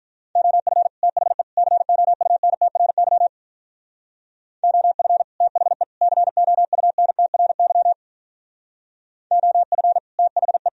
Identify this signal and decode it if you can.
Morse code, heard as a keyed tone, about 43 wpm, 700 Hz. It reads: OF THE COUNTRY OF THE COUNTRY OF THE
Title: Speed-Racing - Sets of 3 Words 35wpm